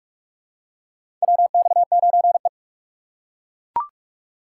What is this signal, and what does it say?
Morse code, heard as a keyed tone, about 42 wpm, 700 Hz. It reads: WX9E
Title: USA Call Signs - 4x Character Spacing - 45wpm